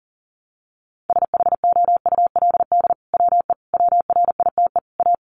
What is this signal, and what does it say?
Morse code, heard as a keyed tone, about 37 wpm, 700 Hz. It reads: SHOULD WE WRITE A
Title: Sentences from Top 1300 Words 40wpm